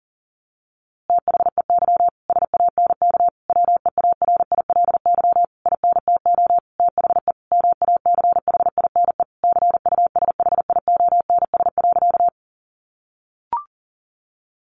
Morse code, heard as a keyed tone, about 40 wpm, 700 Hz. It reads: THEY SANK WEARILY INTO THE MACHINE CUSHIONS.
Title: A Sound Of Thunder 40wpm